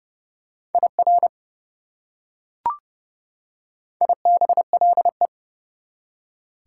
Morse code, heard as a keyed tone, about 30 wpm, 700 Hz. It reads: IR IBLE
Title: Prefixes and Suffixes 30wpm